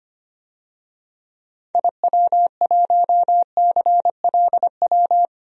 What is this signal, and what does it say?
Morse code, heard as a keyed tone, about 25 wpm, 690 Hz. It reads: IW1CLW
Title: DXCC-100 Call Signs - 6 Characters - 25wpm